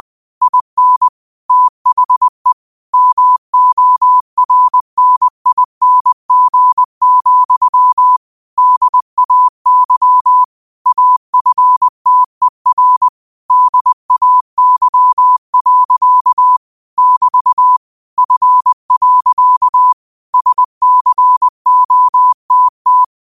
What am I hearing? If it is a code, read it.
Morse code, heard as a keyed tone, about 20 wpm, 1000 Hz. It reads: IN THE MORNING, DAY AFTER DAY. <BT> F. SCOTT